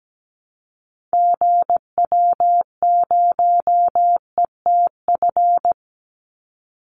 Morse code, heard as a keyed tone, about 17 wpm, 700 Hz. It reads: GW0ETF
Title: Call Signs Repeated 2x - 17wpm